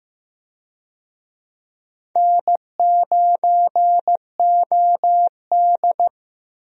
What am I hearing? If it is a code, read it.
Morse code, heard as a keyed tone, about 15 wpm, 700 Hz. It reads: N9OD
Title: USA Call Signs - 4 Characters - Repeated 2x - 15wpm